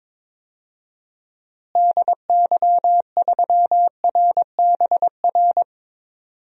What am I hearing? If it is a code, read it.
Morse code, heard as a keyed tone, about 22 wpm, 700 Hz. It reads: DY3RBR
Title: DXCC-100 Call Signs - 6 Characters - Repeated 2x - 22wpm